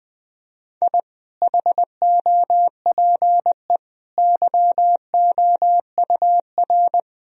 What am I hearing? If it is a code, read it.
Morse code, heard as a keyed tone, about 20 wpm, 700 Hz. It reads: I HOPE YOUR
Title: Sentences from Top 1000 Words - Review Part 3 - 20wpm